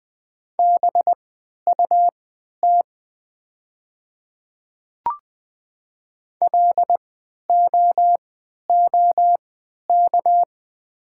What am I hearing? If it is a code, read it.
Morse code, heard as a keyed tone, about 20 wpm, spaced as for 11 wpm, 700 Hz. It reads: BUT LOOK
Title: 3x Character Spacing - Top 200 Words - 20wpm